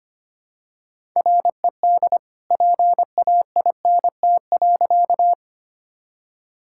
Morse code, heard as a keyed tone, about 25 wpm, 700 Hz. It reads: RED PAINT.
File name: practice / 25wpm / All-Age Appropriate Jokes - 25wpm